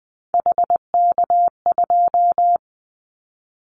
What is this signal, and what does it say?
Morse code, heard as a keyed tone, about 20 wpm, 700 Hz. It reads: HK2
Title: Callsign Prefixes for Top 100 DXCC - 20wpm